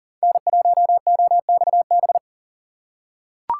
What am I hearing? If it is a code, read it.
Morse code, heard as a keyed tone, about 40 wpm, 700 Hz. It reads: N1OXB
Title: USA Call Signs - 5 Characters - 2x Character Spacing - 40wpm